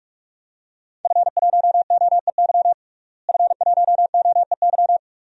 Morse code, heard as a keyed tone, about 46 wpm, 700 Hz. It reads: F1OEY F1OEY
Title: DXCC-100 Call Signs - 5 Characters - Repeated 3x - 45wpm